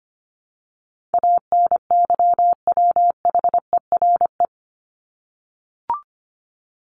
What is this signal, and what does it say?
Morse code, heard as a keyed tone, about 25 wpm, 700 Hz. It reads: ANYWHERE
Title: Top 1251-1500 Words 25wpm